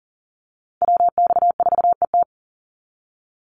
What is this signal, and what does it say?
Morse code, heard as a keyed tone, about 39 wpm, 700 Hz. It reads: WX4ET
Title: NA SOTA Chaser Callsigns 40wpm